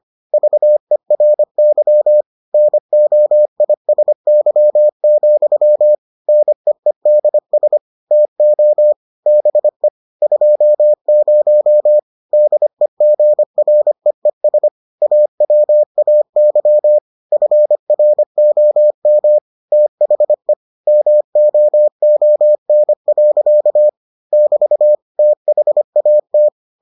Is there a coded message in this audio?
Yes: Morse code from 0.3 to 26.5 s, heard as a keyed tone, about 25 wpm, 600 Hz. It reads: VERY NOISY, NEEDS TO BE 20 DEGREES AWAY FROM THE MOON. <BT> THAT